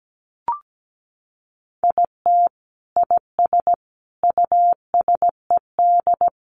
Morse code, heard as a keyed tone, about 17 wpm, 700 Hz. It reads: IT IS USED
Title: Sentences from Top 1100 Words 17wpm